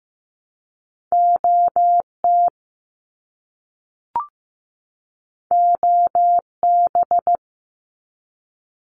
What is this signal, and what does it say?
Morse code, heard as a keyed tone, about 15 wpm, 700 Hz. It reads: OT OB
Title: Abbreviations 15wpm